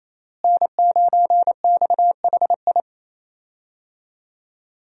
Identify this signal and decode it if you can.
Morse code, heard as a keyed tone, about 27 wpm, 700 Hz. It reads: N9XHI
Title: USA Call Signs - 5 Characters - Repeated 3x - 28wpm